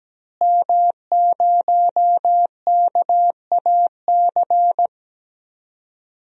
Morse code, heard as a keyed tone, about 17 wpm, 700 Hz. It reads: M0KAC
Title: DXCC-100 Call Signs - 5 Characters - Repeated 3x - 17wpm